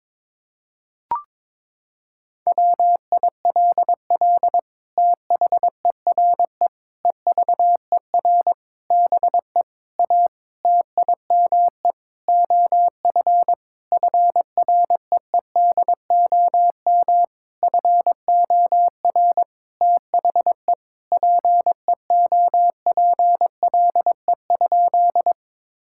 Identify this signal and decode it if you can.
Morse code, heard as a keyed tone, about 22 wpm, 700 Hz. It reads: WILL THERE EVER BE A TIME OF FREEDOM FOR THE PEOPLE?
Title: Sentences from Top 1000 Words - Review Part 5 - 22wpm